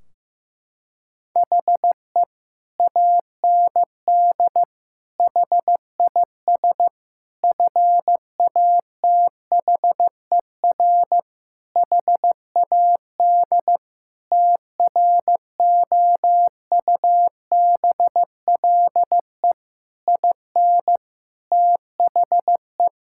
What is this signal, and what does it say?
Morse code, heard as a keyed tone, about 15 wpm, 700 Hz. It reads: HE AND HIS FATHER HAD TROUBLE IN THE
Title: Sentences from Top 1600 Words 15wpm